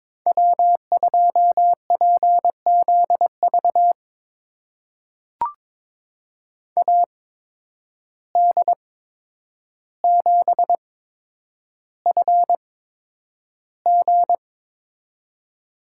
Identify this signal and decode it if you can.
Morse code, heard as a keyed tone, about 22 wpm, spaced as for 9 wpm, 700 Hz. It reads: W2PZV AD7FG
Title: USA Call Signs - 5 Characters - 8x Character Spacing - 22wpm